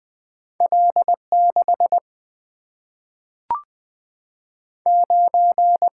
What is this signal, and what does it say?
Morse code, heard as a keyed tone, about 20 wpm, 700 Hz. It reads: L 6 9
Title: Single-Letter Single-Number 20wpm